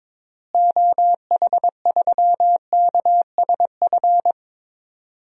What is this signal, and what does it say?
Morse code, heard as a keyed tone, about 22 wpm, 700 Hz. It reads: OH3KSF
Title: DXCC-100 Call Signs - 6 Characters - Repeated 2x - 22wpm